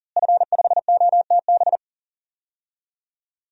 Morse code, heard as a keyed tone, about 39 wpm, 700 Hz. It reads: F5OTB
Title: DXCC-100 Call Signs - 5 Characters - Repeated 3x - 40wpm